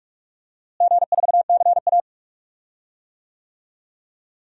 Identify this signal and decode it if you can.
Morse code, heard as a keyed tone, about 45 wpm, 700 Hz. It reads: G4CA
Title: DXCC-100 Call Signs - 4 Characters - Repeated 2x - 45wpm